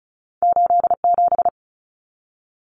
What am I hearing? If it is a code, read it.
Morse code, heard as a keyed tone, about 36 wpm, 700 Hz. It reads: 87
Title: Numbers - 2-Digit 35wpm